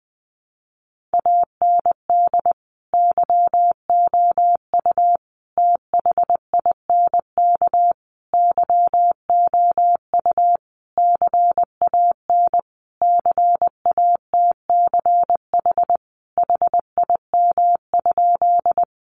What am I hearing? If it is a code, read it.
Morse code, heard as a keyed tone, about 20 wpm, 700 Hz. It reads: AND YOU THINK YOU CAN CATCH HIM?
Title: Sentences from Top 1500 Words 20wpm